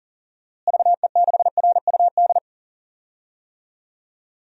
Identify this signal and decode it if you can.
Morse code, heard as a keyed tone, about 40 wpm, 700 Hz. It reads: VE6RUD
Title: DXCC-100 Call Signs - 6 Characters - Repeated 3x - 40wpm